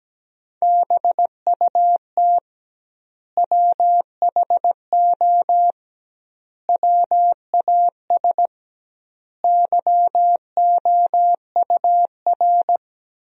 Morse code, heard as a keyed tone, about 17 wpm, 700 Hz. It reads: BUT WHO WAS YOUR
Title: 2x Word-Spacing - Sentences from Top 200 Words - 17wpm